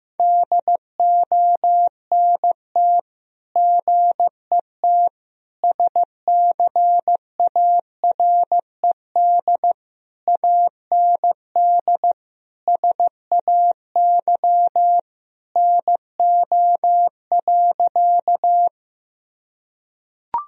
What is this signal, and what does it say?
Morse code, heard as a keyed tone, about 15 wpm, 700 Hz. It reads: DONT GET SCARED AND SAY NO.
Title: Sentences from Top 1000 Words - Review Part 4 - 15wpm